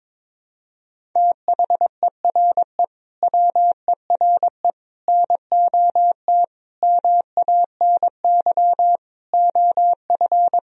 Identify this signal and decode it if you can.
Morse code, heard as a keyed tone, about 22 wpm, 700 Hz. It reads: THERE WERE NOT MANY OF
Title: Sentences from Top 1000 Words - Review Part 1 - 22wpm